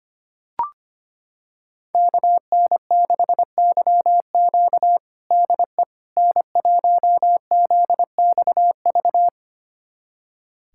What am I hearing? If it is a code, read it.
Morse code, heard as a keyed tone, about 25 wpm, 700 Hz. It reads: KN6YQ DE N1ZXV
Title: USA Callsign DE Callsign - 5 Characters - 25wpm